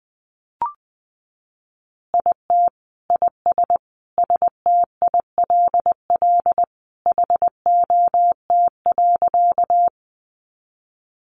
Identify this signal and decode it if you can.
Morse code, heard as a keyed tone, about 19 wpm, 700 Hz. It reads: IT IS STILL HOT.
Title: Additional Easy Phrases 20wpm